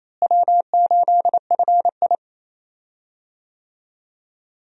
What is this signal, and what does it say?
Morse code, heard as a keyed tone, about 27 wpm, 700 Hz. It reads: W8FI
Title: USA Call Signs - 4 Characters - Repeated 3x - 28wpm